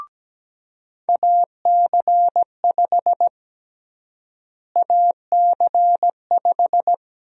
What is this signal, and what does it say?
Morse code, heard as a keyed tone, about 17 wpm, 700 Hz. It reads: AC5 AC5
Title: USA Call Sign Prefixes - Repeated 2x - 17wpm